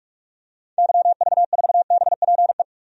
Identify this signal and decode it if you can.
Morse code, heard as a keyed tone, about 46 wpm, 700 Hz. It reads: YV4BPE
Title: DXCC-100 Call Signs - 6 Characters - 45wpm